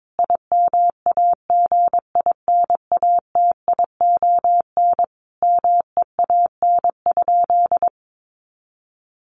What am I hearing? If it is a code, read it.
Morse code, heard as a keyed tone, about 22 wpm, 700 Hz. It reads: IMAGINATION MEAN?
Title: Sentences from Top 1300 Words 22wpm